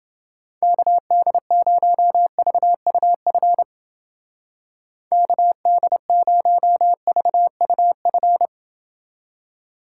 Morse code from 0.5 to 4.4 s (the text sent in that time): KD0VUF